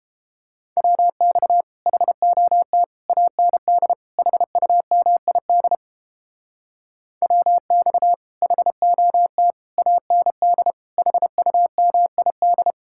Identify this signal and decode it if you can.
Morse code, heard as a keyed tone, about 31 wpm, 700 Hz. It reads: WX HOT AND HUMID WX HOT AND HUMID